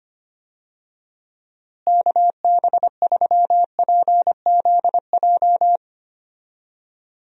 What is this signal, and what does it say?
Morse code, heard as a keyed tone, about 25 wpm, 700 Hz. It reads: KB3PZJ
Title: USA Call Signs - 6 Characters - Repeated 2x - 25wpm